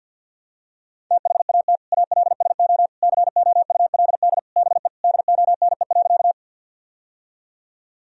Morse code, heard as a keyed tone, about 52 wpm, 700 Hz. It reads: THAT ALSO COULD BE DONE.
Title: Sentences from Top 1000 Words - Review Part 1 - 50wpm